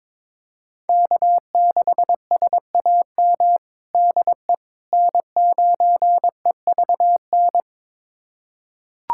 Morse code, heard as a keyed tone, about 22 wpm, 700 Hz. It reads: K6SAM DE N9EVN
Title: USA Callsign DE Callsign - 5 Characters - 22wpm